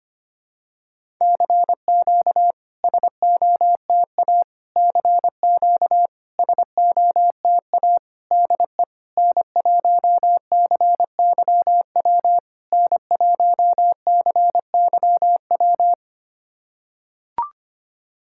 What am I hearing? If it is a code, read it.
Morse code, heard as a keyed tone, about 25 wpm, 690 Hz. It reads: CQ SOTA CQ SOTA DE N1CYW N1CYW
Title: SOTA QSO Practice 25wpm